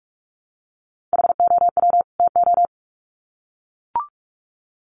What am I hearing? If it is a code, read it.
Morse code, heard as a keyed tone, about 44 wpm, 700 Hz. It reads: HOW TO
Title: Sets of 2 Words 45wpm